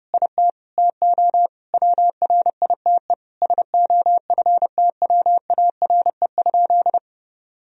Morse code, heard as a keyed tone, about 30 wpm, 700 Hz. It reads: IT TO WRITE SOFTWARE?
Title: Sentences from Top 1000 Words - Review Part 3 - 30wpm